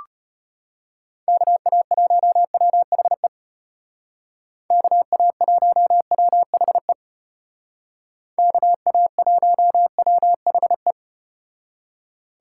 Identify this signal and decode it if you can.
Morse code, heard as a keyed tone, about 34 wpm, 700 Hz. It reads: KA1WHE KA1WHE KA1WHE